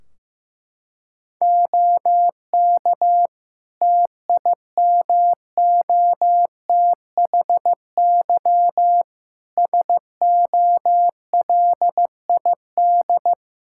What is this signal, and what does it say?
Morse code, heard as a keyed tone, about 16 wpm, 700 Hz. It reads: OK TIMOTHY SOLID
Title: K1USN SST QSO Practice 15wpm